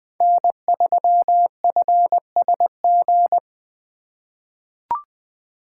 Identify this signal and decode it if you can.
Morse code, heard as a keyed tone, about 20 wpm, 700 Hz. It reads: N3FSG
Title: USA Call Signs - 5 Characters - 3x Character Spacing - 20wpm